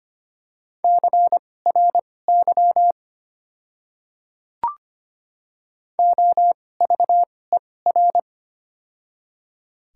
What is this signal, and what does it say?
Morse code, heard as a keyed tone, about 25 wpm, spaced as for 18 wpm, 700 Hz. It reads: CRY OVER
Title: 2x Character Spacing - Top 500 Words - 25wpm